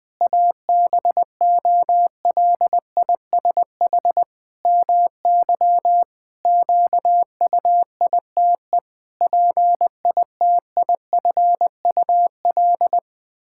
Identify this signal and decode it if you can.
Morse code, heard as a keyed tone, about 20 wpm, 700 Hz. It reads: ABOLISH MY QUITE PITIFUL
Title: Pangram Challenge v2 - 20wpm